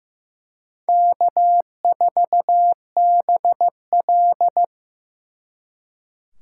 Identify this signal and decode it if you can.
Morse code, heard as a keyed tone, about 18 wpm, 700 Hz. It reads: K4BL
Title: Speed Racing - USA 1x2 Call Signs 15wpm